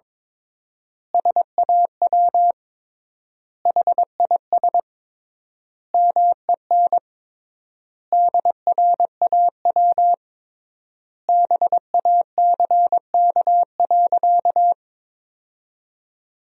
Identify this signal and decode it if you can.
Morse code, heard as a keyed tone, about 22 wpm, 700 Hz. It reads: SAW HIS MEN DRAW BACK.
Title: 3x Word-Spacing - Sentences from Top 300 Words - 22wpm